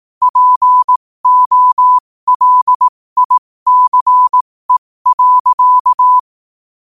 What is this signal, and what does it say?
Morse code, heard as a keyed tone, about 18 wpm, spaced as for 15 wpm, 1000 Hz. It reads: POLICE.
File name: qotdfiles / 15wpm